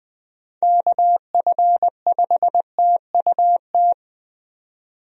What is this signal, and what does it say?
Morse code, heard as a keyed tone, about 21 wpm, 700 Hz. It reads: KF5TUT
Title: USA Call Signs - 6 Characters - 20wpm